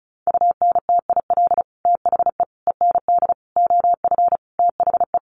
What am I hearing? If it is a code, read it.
Morse code, heard as a keyed tone, about 34 wpm, 700 Hz. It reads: UNTIL THE END OF THE
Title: Sets of 5 Words Encore 35wpm